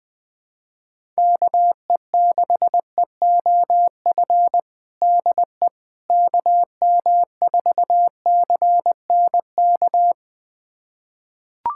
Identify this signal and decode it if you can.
Morse code, heard as a keyed tone, about 20 wpm, 700 Hz. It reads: KE6EOF DE KM4CNK
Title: USA Callsign DE Callsign - 6 Characters - 20wpm